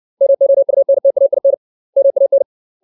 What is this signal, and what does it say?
Morse code, heard as a keyed tone, about 60 wpm, 550 Hz. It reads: MOUNTAIN MAN
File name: 632 words / 632 Words 60WPM